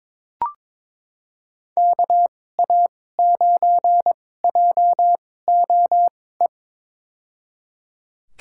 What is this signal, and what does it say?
Morse code, heard as a keyed tone, about 22 wpm, spaced as for 16 wpm, 700 Hz. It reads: KA9JOE
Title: USA Call Signs - 2x Character Spacing - 22wpm